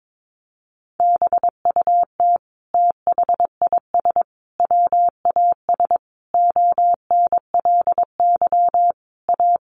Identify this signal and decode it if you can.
Morse code, heard as a keyed tone, about 22 wpm, 700 Hz. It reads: BUT THIS WAS ONLY A
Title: Sentences from Top 1500 Words 22wpm